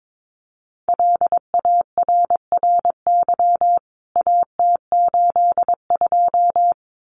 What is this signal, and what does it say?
Morse code, heard as a keyed tone, about 22 wpm, 700 Hz. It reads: LARRY AT82
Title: CWT Exchange 22wpm